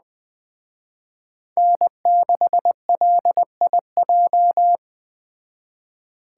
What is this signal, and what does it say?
Morse code, heard as a keyed tone, about 20 wpm, 700 Hz. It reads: N6LIJ